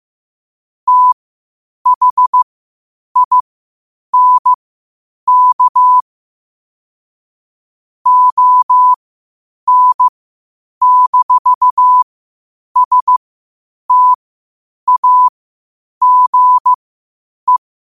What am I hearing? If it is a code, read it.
Morse code, heard as a keyed tone, about 15 wpm, spaced as for 8 wpm, 1000 Hz. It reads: THINK ON-STAGE
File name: qotdfiles / 8wpm